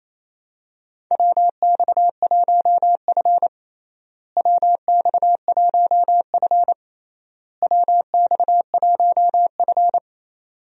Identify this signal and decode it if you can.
Morse code, heard as a keyed tone, about 28 wpm, 700 Hz. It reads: WX1F WX1F WX1F
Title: USA Call Signs - 4 Characters - Repeated 3x - 28wpm